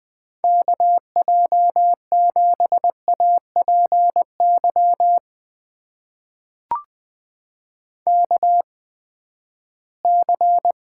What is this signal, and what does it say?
Morse code, heard as a keyed tone, about 20 wpm, spaced as for 14 wpm, 700 Hz. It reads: KJ7APY KC
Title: USA Call Signs - 6 Characters - 8x Character Spacing - 20wpm